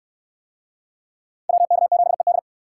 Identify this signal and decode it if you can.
Morse code, heard as a keyed tone, about 69 wpm, 700 Hz. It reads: WK7ED